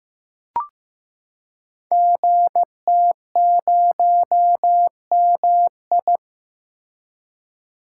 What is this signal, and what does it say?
Morse code, heard as a keyed tone, about 15 wpm, 700 Hz. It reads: GT0MI
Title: Code Groups - 5 Letter:Numbers - 15wpm